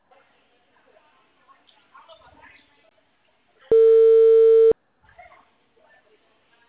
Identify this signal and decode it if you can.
Ringback tone (China, 450 Hz), with speech.
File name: out- -2006- - - .23